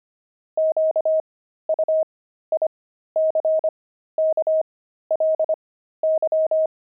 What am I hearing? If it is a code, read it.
Morse code, heard as a keyed tone, about 25 wpm, spaced as for 13 wpm, 630 Hz. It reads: QUICKLY
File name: pangram-fs-13-cs-25-devils